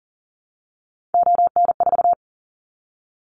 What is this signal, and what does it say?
Morse code, heard as a keyed tone, about 40 wpm, 700 Hz. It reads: ON4
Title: Callsign Prefixes for Top 25 DXCC - 40wpm